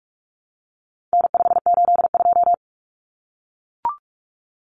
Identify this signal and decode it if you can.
Morse code, heard as a keyed tone, about 44 wpm, 700 Hz. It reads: N582
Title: Common Cut Numbers - 4-Digit 45wpm